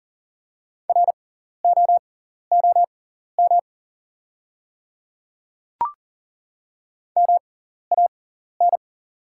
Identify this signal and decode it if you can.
Morse code, heard as a keyed tone, about 40 wpm, spaced as for 14 wpm, 690 Hz. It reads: ROOM MAN
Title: 6x Character Spacing - Top 300 Words - 40wpm